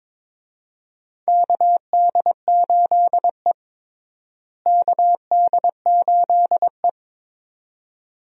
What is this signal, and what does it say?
Morse code, heard as a keyed tone, about 21 wpm, 700 Hz. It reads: KD8E KD8E
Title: USA Call Signs - 4 Characters - Repeated 2x - 22wpm